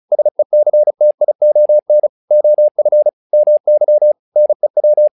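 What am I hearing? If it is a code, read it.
Morse code, heard as a keyed tone, about 36 wpm, 600 Hz. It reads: SECTION OF MY NEW
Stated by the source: Morse practice text "Tower Trouble" sent at 35 WPM, from the SKCC - Straight Key Century Club learning center audio files.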